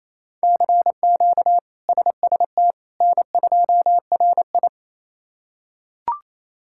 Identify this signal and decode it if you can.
Morse code, heard as a keyed tone, about 28 wpm, 700 Hz. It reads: CQ SST N2RI
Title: K1USN SST QSO Practice - 28wpm